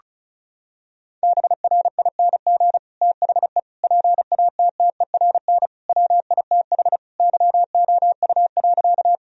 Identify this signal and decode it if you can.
Morse code, heard as a keyed tone, about 34 wpm, 700 Hz. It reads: BRING THE PATTERN WITH YOU.
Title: Sentences from Top 1000 Words - Review Part 2 - 35wpm